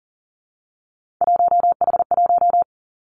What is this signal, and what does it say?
Morse code, heard as a keyed tone, about 42 wpm, 700 Hz. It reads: 1H1
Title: Three Challenging Letter-Number Combinations 40wpm